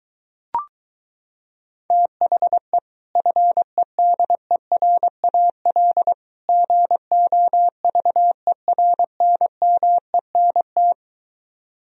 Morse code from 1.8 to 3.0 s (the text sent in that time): THE